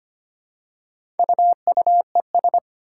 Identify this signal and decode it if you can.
Morse code, heard as a keyed tone, about 25 wpm, 700 Hz. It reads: UUES
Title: Four Challenging Letter-Number Combinations 25wpm